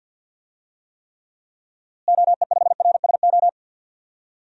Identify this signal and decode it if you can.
Morse code, heard as a keyed tone, about 53 wpm, 700 Hz. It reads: OE5RSO
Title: DXCC-100 Call Signs - 6 Characters - Repeated 2x - 50wpm